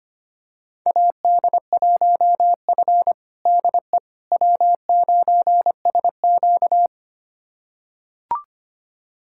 Morse code, heard as a keyed tone, about 25 wpm, 700 Hz. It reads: AD1F DE W9SQ
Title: USA Callsign DE Callsign - 4 Characters - 25wpm